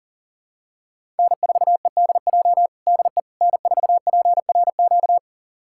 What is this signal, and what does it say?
Morse code, heard as a keyed tone, about 37 wpm, 700 Hz. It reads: N4EDJ DE N4PRQ
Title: USA Callsign DE Callsign - 5 Characters - 40wpm